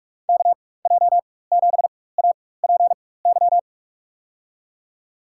Morse code, heard as a keyed tone, about 46 wpm, spaced as for 21 wpm, 700 Hz. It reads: KJ7APY
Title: USA Call Signs - 4x Character Spacing - 45wpm